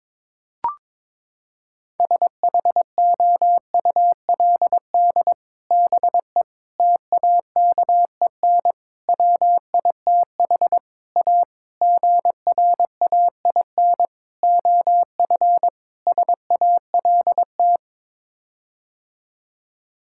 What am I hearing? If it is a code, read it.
Morse code, heard as a keyed tone, about 22 wpm, 700 Hz. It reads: SHOULD BE TAKEN WITH A GRAIN OF SALT
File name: practice / 22wpm / Sets of 8 Words - 22wpm